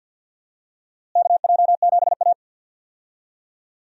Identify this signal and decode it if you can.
Morse code, heard as a keyed tone, about 46 wpm, 700 Hz. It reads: KJ7A
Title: USA Call Signs - 4 Characters - Repeated 3x - 50wpm